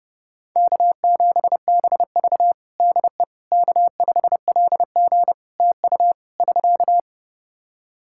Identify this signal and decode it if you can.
Morse code, heard as a keyed tone, about 30 wpm, 700 Hz. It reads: K7BV DE K5LG TU <SK>
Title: K1USN SST QSO Practice 30wpm